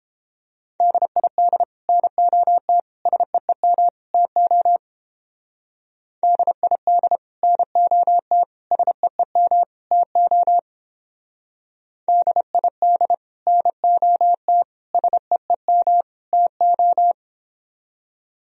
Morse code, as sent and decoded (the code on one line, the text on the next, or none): -.. .. -.. / -. --- - / ... . -- / - --- / -.. .. -.. / -. --- - / ... . -- / - --- / -.. .. -.. / -. --- - / ... . -- / - ---
DID NOT SEEM TO DID NOT SEEM TO DID NOT SEEM TO